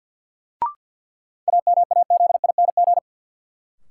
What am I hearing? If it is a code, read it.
Morse code, heard as a keyed tone, about 53 wpm, 700 Hz. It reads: AMAZING
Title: Sentences from Top 1500 Words - New Word Drill - 50wpm